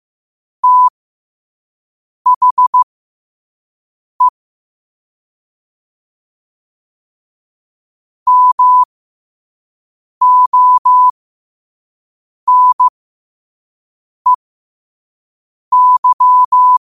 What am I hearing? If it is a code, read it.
Morse code, heard as a keyed tone, about 15 wpm, spaced as for 5 wpm, 1000 Hz. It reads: THE MONEY
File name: qotdfiles / 5wpm